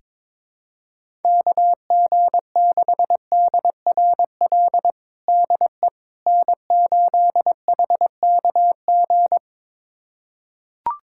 Morse code, heard as a keyed tone, about 23 wpm, 700 Hz. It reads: KG6DRL DE N8HKG
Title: USA Callsign DE Callsign 22wpm